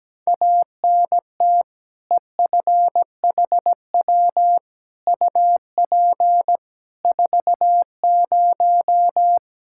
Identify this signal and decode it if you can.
Morse code, heard as a keyed tone, about 17 wpm, 700 Hz. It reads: ANT EFHW UP 40
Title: K1USN SST QSO Practice 17wpm